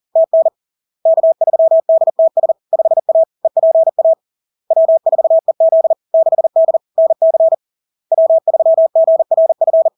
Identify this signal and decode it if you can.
Morse code, heard as a keyed tone, about 40 wpm, 650 Hz. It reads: TN K3DTS 5A EPA W4EZ 6D NC W3GRF
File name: LARK-FD-2022-Log-in-CW-40wpm-650-Hz